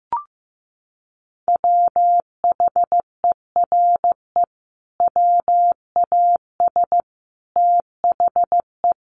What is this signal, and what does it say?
Morse code, heard as a keyed tone, about 15 wpm, 700 Hz. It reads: WHERE WAS THE
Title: Sentences from Top 1500 Words 15wpm